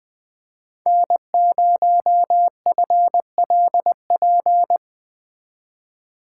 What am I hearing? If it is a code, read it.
Morse code, heard as a keyed tone, about 20 wpm, 700 Hz. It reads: N0FLP
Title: Speed Racing - USA 1x3 Call Signs 17wpm